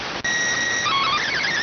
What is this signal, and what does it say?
SSTV (slow-scan television), PD 290.